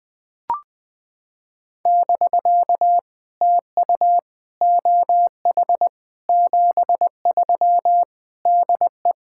Morse code, heard as a keyed tone, about 20 wpm, 700 Hz. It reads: <BK> TU OH 73 DE
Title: POTA QSO Practice - 20wpm